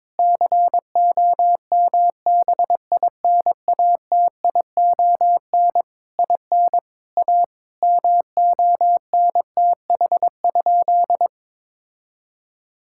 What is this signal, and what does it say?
Morse code, heard as a keyed tone, about 22 wpm, 700 Hz. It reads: COMBINATION IN A MONTH?
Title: Sentences from Top 1000 Words - Review Part 6 - 22wpm